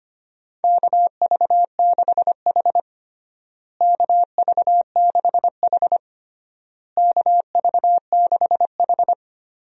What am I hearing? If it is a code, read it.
Morse code, heard as a keyed tone, about 25 wpm, 700 Hz. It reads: KV6H KV6H KV6H
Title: USA Call Signs - 4 Characters - Repeated 3x - 25wpm